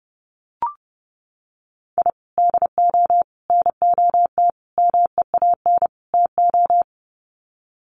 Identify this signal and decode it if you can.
Morse code, heard as a keyed tone, about 30 wpm, 700 Hz. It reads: I DO NOT MEAN TO
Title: Sets of 5 Words 30wpm